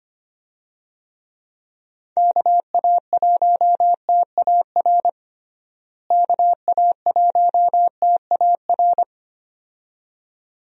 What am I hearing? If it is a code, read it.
Morse code, heard as a keyed tone, about 25 wpm, 700 Hz. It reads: KA1TAR KA1TAR
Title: USA Call Signs - 6 Characters - Repeated 2x - 25wpm